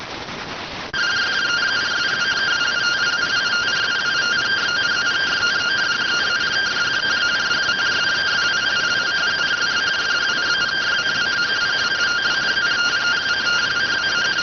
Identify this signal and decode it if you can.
Two-tone teleprinter signal, about 170 Hz shift.